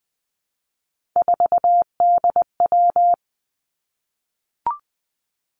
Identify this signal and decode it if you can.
Morse code, heard as a keyed tone, about 20 wpm, 700 Hz. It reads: 4DW
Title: Three Challenging Letter-Number Combinations 20wpm